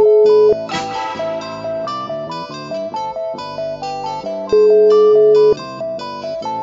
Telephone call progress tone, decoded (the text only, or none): ringback tone